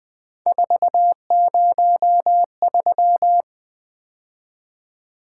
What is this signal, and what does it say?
Morse code, heard as a keyed tone, about 20 wpm, 700 Hz. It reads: 403
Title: Three Letter-Number Combinations - 20wpm